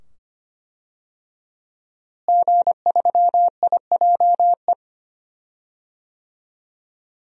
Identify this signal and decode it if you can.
Morse code, heard as a keyed tone, about 25 wpm, 700 Hz. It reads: G3IJE
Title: DXCC-100 Call Signs - 5 Characters - Repeated 2x - 25wpm